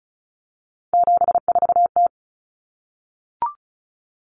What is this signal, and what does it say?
Morse code, heard as a keyed tone, about 34 wpm, 700 Hz. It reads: 74T
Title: Common Cut Numbers - 3-Digit 35wpm